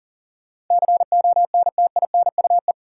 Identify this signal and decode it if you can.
Morse code, heard as a keyed tone, about 40 wpm, 700 Hz. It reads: CONTINUE
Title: Sentences from Top 1000 Words - Review Part 6 - New Word Drill - 40wpm